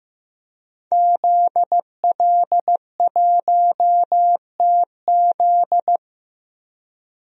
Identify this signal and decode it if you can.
Morse code, heard as a keyed tone, about 15 wpm, 700 Hz. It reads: ZL1TZ
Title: DXCC-100 Call Signs - 5 Characters - 15wpm